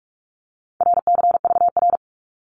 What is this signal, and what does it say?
Morse code, heard as a keyed tone, about 44 wpm, 700 Hz. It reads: RCVR
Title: Abbreviations 45wpm